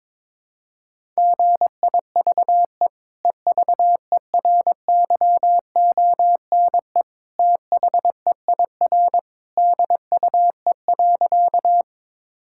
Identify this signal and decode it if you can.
Morse code, heard as a keyed tone, about 22 wpm, 700 Hz. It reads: GIVE EVERYONE THEIR DUE.
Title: Sentences from Top 1000 Words - Review Part 6 - 22wpm